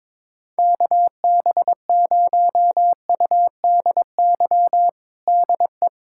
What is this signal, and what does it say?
Morse code, heard as a keyed tone, about 22 wpm, 700 Hz. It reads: KB0UDY DE
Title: USA Callsign DE Callsign - 6 Characters - 22wpm